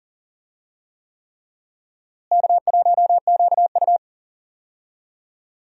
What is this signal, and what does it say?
Morse code, heard as a keyed tone, about 40 wpm, 700 Hz. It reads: K1QU
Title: USA Call Signs - 4 Characters - Repeated 3x - 40wpm